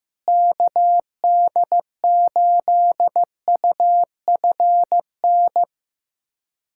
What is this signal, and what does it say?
Morse code, heard as a keyed tone, about 16 wpm, 700 Hz. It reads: KD8UFN
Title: USA Callsign DE Callsign - 6 Characters - 15wpm